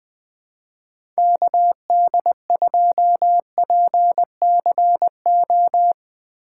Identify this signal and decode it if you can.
Morse code, heard as a keyed tone, about 20 wpm, 700 Hz. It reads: KD2PCO
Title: USA Call Signs - 6x Character Spacing - 20wpm